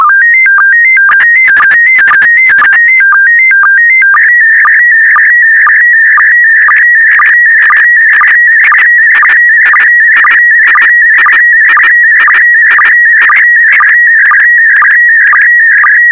SSTV (slow-scan television), PD 120.
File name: sstv / PD120-VK4MSL